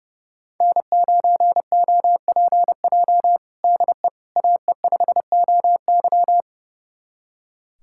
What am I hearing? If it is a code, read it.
Morse code, heard as a keyed tone, about 32 wpm, 700 Hz. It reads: N9OPJ DE AE5OY
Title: USA Callsign DE Callsign - 5 Characters - 30wpm